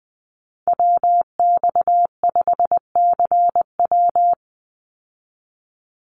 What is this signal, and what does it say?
Morse code, heard as a keyed tone, about 20 wpm, 700 Hz. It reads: WX5CW
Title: Call Signs Repeated 3x - 20wpm